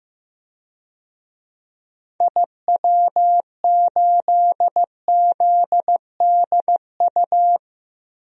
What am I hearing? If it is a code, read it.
Morse code, heard as a keyed tone, about 15 wpm, 700 Hz. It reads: IW8ZDU
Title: DXCC-100 Call Signs - 6 Characters - 15wpm